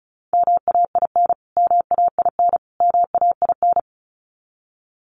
Morse code, heard as a keyed tone, about 34 wpm, 700 Hz. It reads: MAIN MAIN MAIN
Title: Top 300 Words Repeated 35wpm